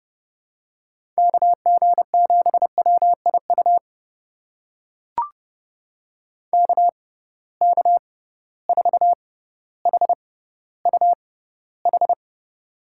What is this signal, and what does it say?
Morse code, heard as a keyed tone, about 30 wpm, spaced as for 15 wpm, 700 Hz. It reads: KG7WIU KK4HUH
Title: USA Call Signs - 6 Characters - 6x Character Spacing - 30wpm